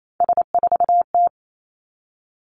Morse code, heard as a keyed tone, about 28 wpm, 700 Hz. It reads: S4T
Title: Three Challenging Letter-Number Combinations 28wpm